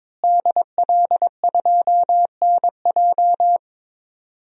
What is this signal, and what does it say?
Morse code, heard as a keyed tone, about 22 wpm, 700 Hz. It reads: DL2NJ
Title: DXCC-100 Call Signs - 5 Characters - Repeated 2x - 22wpm